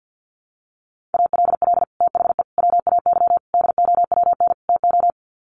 Morse code, heard as a keyed tone, about 50 wpm, 700 Hz. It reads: ALL THE WAY DOWN TO
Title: Sets of 5 Words Encore2 50wpm